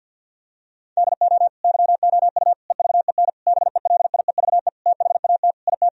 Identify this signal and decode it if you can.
Morse code, heard as a keyed tone, about 49 wpm, 700 Hz. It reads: DO YOU EVEN BELIEVE THAT IT
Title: Sentences from Top 1000 Words - Review Part 1 - 50wpm